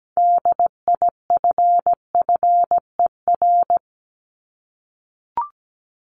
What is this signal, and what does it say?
Morse code, heard as a keyed tone, about 17 wpm, 700 Hz. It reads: DIFFER
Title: Top 500 Words 17wpm